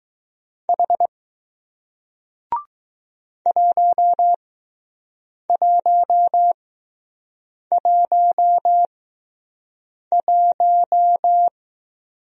Morse code, heard as a keyed tone, about 21 wpm, spaced as for 6 wpm, 700 Hz. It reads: H 1111